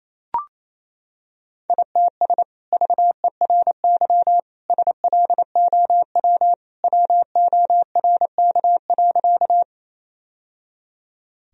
Morse code, heard as a keyed tone, about 28 wpm, 700 Hz. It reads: ITS VERY SLOW WORK.
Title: Sentences from Top 1000 Words - Review Part 2 - 28wpm